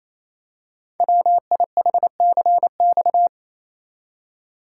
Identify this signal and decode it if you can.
Morse code, heard as a keyed tone, about 27 wpm, 700 Hz. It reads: WIHCX
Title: Code Groups - 5 Letters - 28wpm